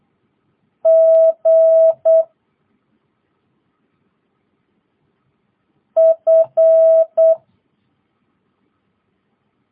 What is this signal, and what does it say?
Morse code, heard as a keyed tone, about 8 wpm, 650 Hz. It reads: G F
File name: jh8cbh / 2022cw